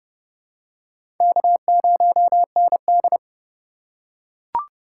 Morse code, heard as a keyed tone, about 32 wpm, 700 Hz. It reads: K0ND
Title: USA Call Signs - 4x Character Spacing - 30wpm